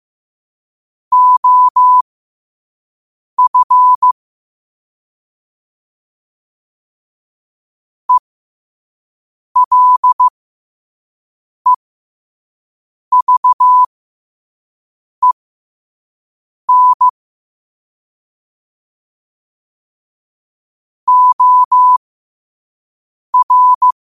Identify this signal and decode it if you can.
Morse code, heard as a keyed tone, about 15 wpm, spaced as for 5 wpm, 1000 Hz. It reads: OF ELEVEN OR
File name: qotdfiles / 5wpm